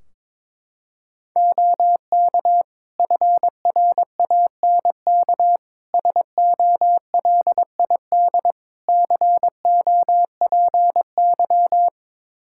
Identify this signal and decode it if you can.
Morse code, heard as a keyed tone, about 22 wpm, 700 Hz. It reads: OK FRANK SOLID COPY
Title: Practice QSO - Exchange 3 - More Info - 22wpm